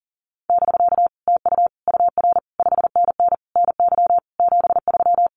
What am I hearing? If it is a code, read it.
Morse code, heard as a keyed tone, about 41 wpm, 700 Hz. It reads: <BK> TU UR 5NN NY 73
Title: Common QSO Phrases Encore 40wpm